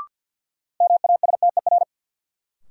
Morse code, heard as a keyed tone, about 53 wpm, 700 Hz. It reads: MASTER
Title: Sentences from Top 1300 Words - New Word Drill - 50wpm